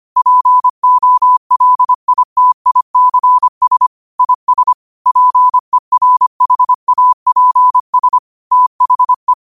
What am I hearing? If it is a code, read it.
Morse code, heard as a keyed tone, about 25 wpm, 1000 Hz. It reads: POLITICS IS PERHAPS THE
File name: qotdfiles / 25wpm